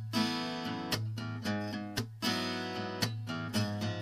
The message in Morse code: none